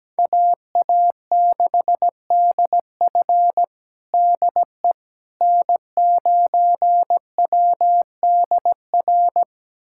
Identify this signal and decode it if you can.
Morse code, heard as a keyed tone, about 17 wpm, 700 Hz. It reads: AA6DF DE N9WDR
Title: USA Callsign DE Callsign - 5 Characters - 17wpm